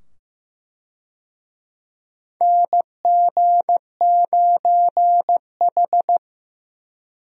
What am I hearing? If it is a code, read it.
Morse code, heard as a keyed tone, about 15 wpm, 700 Hz. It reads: NG9H
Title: USA Call Signs - 4 Characters - 15wpm